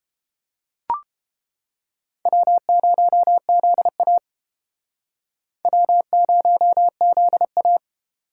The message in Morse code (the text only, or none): W0ZA W0ZA